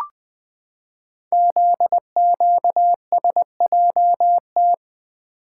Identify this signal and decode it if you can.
Morse code, heard as a keyed tone, about 20 wpm, 700 Hz. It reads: ZQSJT
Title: Code Groups - 5 Letters - 20wpm